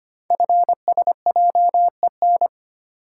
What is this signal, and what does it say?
Morse code, heard as a keyed tone, about 25 wpm, 700 Hz. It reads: FSJEN
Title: Code Groups - 5 Letters - 25wpm